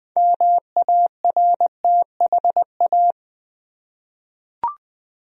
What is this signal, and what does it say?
Morse code, heard as a keyed tone, about 20 wpm, 700 Hz. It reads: MARTHA
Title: USA Common Names - 20wpm